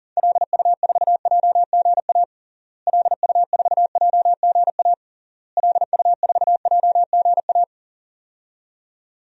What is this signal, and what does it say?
Morse code, heard as a keyed tone, about 40 wpm, 700 Hz. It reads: LU4JGA LU4JGA LU4JGA
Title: DXCC-100 Call Signs - 6 Characters - Repeated 3x - 40wpm